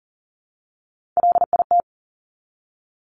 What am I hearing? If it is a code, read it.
Morse code, heard as a keyed tone, about 40 wpm, 690 Hz. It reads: LIT
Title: Top Three Letter Combinations in Words 40wpm